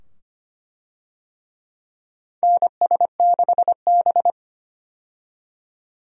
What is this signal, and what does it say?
Morse code, heard as a keyed tone, about 25 wpm, 700 Hz. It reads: NS6B
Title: USA Call Signs - 4 Characters - 25wpm